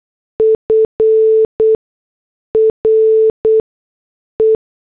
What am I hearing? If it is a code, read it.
Morse code, heard as a keyed tone, about 8 wpm, 440 Hz. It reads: F R E
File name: morse / headlines